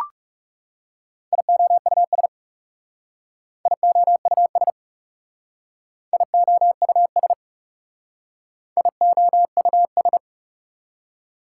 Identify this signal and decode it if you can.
Morse code, heard as a keyed tone, about 37 wpm, 700 Hz. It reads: IOUS IOUS IOUS IOUS
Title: Speed-Racing Prefixes and Suffixes 30wpm